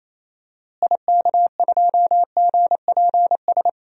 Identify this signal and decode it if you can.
Morse code, heard as a keyed tone, about 28 wpm, 700 Hz. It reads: IK2GPS
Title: DXCC-100 Call Signs - 6 Characters - Repeated 2x - 28wpm